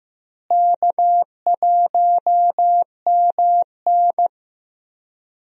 Morse code, heard as a keyed tone, about 15 wpm, 700 Hz. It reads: K1MN
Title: USA Call Signs - 4 Characters - Repeated 3x - 15wpm